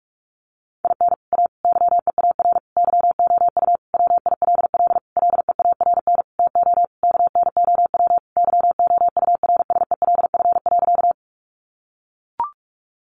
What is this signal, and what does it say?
Morse code, heard as a keyed tone, about 44 wpm, 700 Hz. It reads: IN A YEAR YOU WILL LEARN TO KNOW YOURSELF.
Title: Sentences from Top 600 Words 45wpm